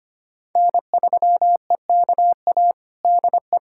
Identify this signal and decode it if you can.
Morse code, heard as a keyed tone, about 25 wpm, 700 Hz. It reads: N3EKA DE
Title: USA Callsign DE Callsign - 5 Characters - 25wpm